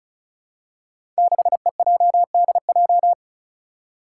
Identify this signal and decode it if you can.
Morse code, heard as a keyed tone, about 36 wpm, 700 Hz. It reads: 6EJDJ
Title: Five Challenging Letter-Number Combinations 35wpm